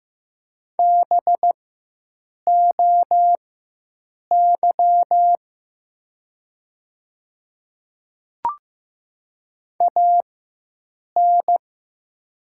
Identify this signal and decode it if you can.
Morse code, heard as a keyed tone, about 15 wpm, spaced as for 7 wpm, 700 Hz. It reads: BOY AN
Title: 4x Character Spacing - Top 200 Words - 15wpm